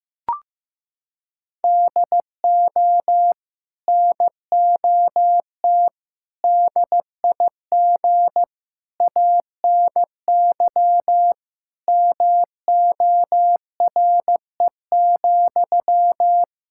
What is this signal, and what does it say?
Morse code, heard as a keyed tone, about 15 wpm, 700 Hz. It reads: DO NOT DIG ANY MORE,
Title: Sentences from Top 1600 Words 15wpm